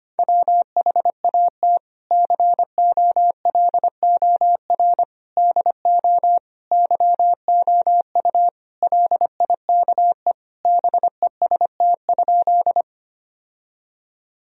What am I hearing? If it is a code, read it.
Morse code, heard as a keyed tone, about 25 wpm, 700 Hz. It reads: WHAT COLOR DO YOU LIKE BEST?
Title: Sentences from Top 1000 Words - Review Part 2 - 25wpm